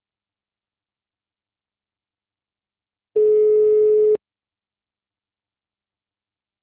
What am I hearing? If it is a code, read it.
Ringback tone (Europe, 425 Hz).